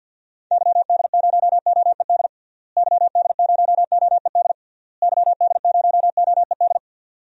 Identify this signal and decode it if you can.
Morse code, heard as a keyed tone, about 45 wpm, 700 Hz. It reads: YD0OED YD0OED YD0OED
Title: DXCC-100 Call Signs - 6 Characters - Repeated 3x - 50wpm